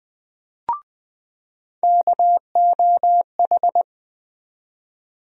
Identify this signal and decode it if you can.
Morse code, heard as a keyed tone, about 20 wpm, 700 Hz. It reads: KOH
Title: USA Call Sign Postfixes - 3 Letters 20wpm